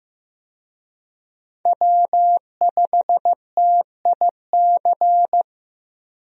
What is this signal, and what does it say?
Morse code, heard as a keyed tone, about 15 wpm, 700 Hz. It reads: W5TIC
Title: USA Call Signs - 5 Characters - Repeated 2x - 15wpm